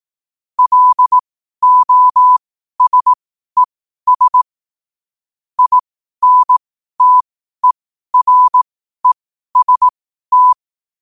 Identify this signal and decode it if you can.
Morse code, heard as a keyed tone, about 18 wpm, spaced as for 12 wpm, 1000 Hz. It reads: LOSES INTEREST
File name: qotdfiles / 12wpm